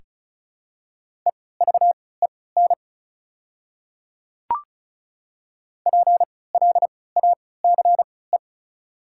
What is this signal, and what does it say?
Morse code, heard as a keyed tone, about 34 wpm, spaced as for 20 wpm, 690 Hz. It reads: EVEN PLACE
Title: 3x Character Spacing - Top 300 Words - 35wpm